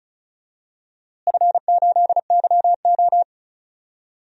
Morse code, heard as a keyed tone, about 36 wpm, 700 Hz. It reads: F8YO
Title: DXCC-100 Call Signs - 4 Characters - Repeated 2x - 35wpm